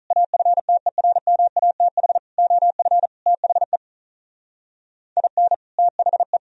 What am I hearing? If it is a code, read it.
Morse code, heard as a keyed tone, about 43 wpm, 700 Hz. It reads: AFTERMATH OF THE IN THE
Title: Speed-Racing - Sets of 5 Words 35wpm